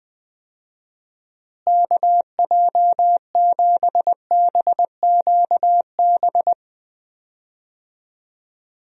Morse code, heard as a keyed tone, about 20 wpm, 700 Hz. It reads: KJ7BQB
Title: USA Call Signs - 6 Characters - Repeated 3x - 20wpm